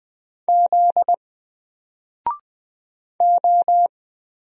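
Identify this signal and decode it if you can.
Morse code, heard as a keyed tone, about 20 wpm, 700 Hz. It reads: Z O
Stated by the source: Single letter and number Morse drill at 20 wpm, from N4LCD.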